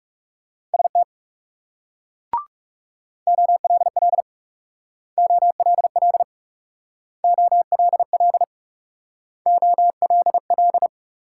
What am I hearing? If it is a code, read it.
Morse code, heard as a keyed tone, about 38 wpm, 700 Hz. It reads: ST OLL OLL OLL OLL